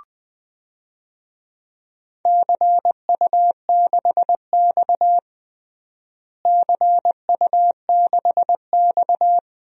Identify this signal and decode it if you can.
Morse code, heard as a keyed tone, about 20 wpm, 700 Hz. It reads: CU6X CU6X
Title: DXCC-100 Call Signs - 4 Characters - Repeated 3x - 20wpm